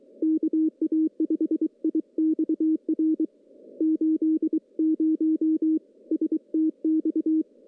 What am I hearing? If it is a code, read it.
Morse code, heard as a keyed tone, about 23 wpm, 320 Hz. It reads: KA5IXR 80 STX
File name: contests / CW-SS-k5tr- -1500